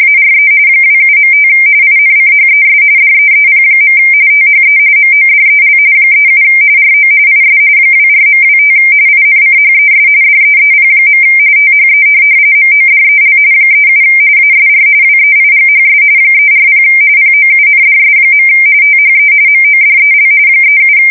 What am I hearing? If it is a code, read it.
RTTY, about 45.45 baud. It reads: ARE 27.5 INCHES DEEP. THE MODEL 37 RO IS 24.25 INCHES DEEP. THE MODEL 37 ASR WEIGHS 340 POUNDS. THE MODEL 37
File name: fsk170 / teletype-corp